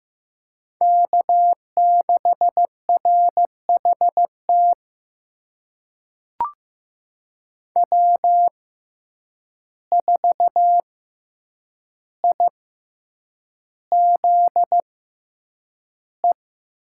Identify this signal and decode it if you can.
Morse code, heard as a keyed tone, about 15 wpm, spaced as for 8 wpm, 700 Hz. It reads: K6RHT W4IZE
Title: USA Call Signs - 5 Characters - 6x Character Spacing - 15wpm